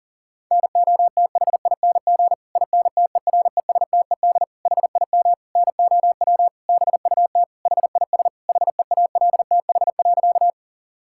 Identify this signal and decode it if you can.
Morse code, heard as a keyed tone, about 37 wpm, 700 Hz. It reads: NOTHING INTERESTED HIM NOW BUT HIS HEALTH.
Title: Sentences from Top 1000 Words - Review Part 6 - 40wpm